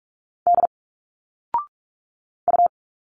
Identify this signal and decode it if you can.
Morse code, heard as a keyed tone, about 46 wpm, 700 Hz. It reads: D U
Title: Single Letters 45wpm